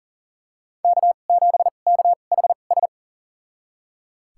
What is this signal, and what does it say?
Morse code, heard as a keyed tone, about 40 wpm, spaced as for 29 wpm, 700 Hz. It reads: K7KHS
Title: USA Call Signs - 5 Characters - 2x Character Spacing - 40wpm